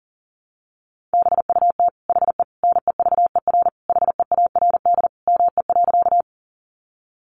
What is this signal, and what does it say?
Morse code, heard as a keyed tone, about 40 wpm, 700 Hz. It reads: BUT HE NEVER HEARD ME.
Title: Sentences from Top 400 Words 40wpm